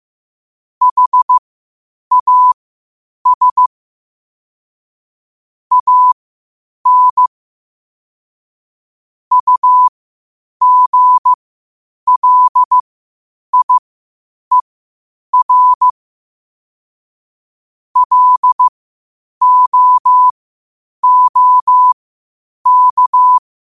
Morse code, heard as a keyed tone, about 15 wpm, spaced as for 8 wpm, 1000 Hz. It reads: HAS AN UGLIER LOOK
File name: qotdfiles / 8wpm